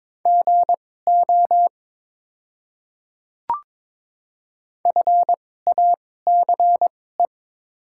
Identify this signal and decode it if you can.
Morse code, heard as a keyed tone, about 21 wpm, spaced as for 16 wpm, 700 Hz. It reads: GO FACE
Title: 2x Character Spacing - Top 300 Words - 22wpm